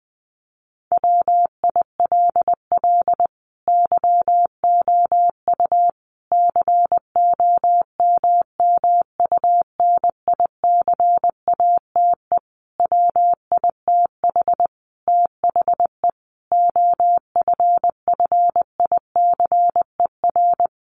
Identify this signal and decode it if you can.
Morse code, heard as a keyed tone, about 20 wpm, 700 Hz. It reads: WILL YOU COMMUNICATE WITH THE OFFICER
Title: Sentences from Top 1300 Words 20wpm